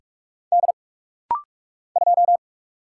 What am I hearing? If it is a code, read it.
Morse code, heard as a keyed tone, about 46 wpm, 700 Hz. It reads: D 2
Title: Single Letter-Number Alt - Rapid-Fire 45wpm